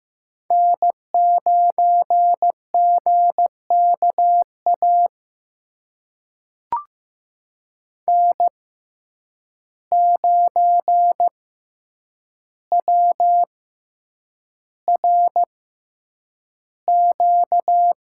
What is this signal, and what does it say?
Morse code, heard as a keyed tone, about 15 wpm, spaced as for 8 wpm, 700 Hz. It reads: N9GKA N9WRQ
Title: USA Call Signs - 5 Characters - 6x Character Spacing - 15wpm